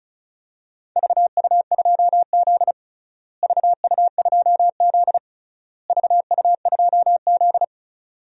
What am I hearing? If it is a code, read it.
Morse code, heard as a keyed tone, about 34 wpm, 700 Hz. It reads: VU2Z VU2Z VU2Z
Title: DXCC-100 Call Signs - 4 Characters - Repeated 3x - 35wpm